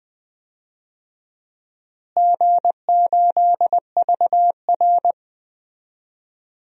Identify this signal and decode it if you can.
Morse code, heard as a keyed tone, about 20 wpm, 700 Hz. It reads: G8VR
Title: DXCC-100 Call Signs - 4 Characters - Repeated 3x - 20wpm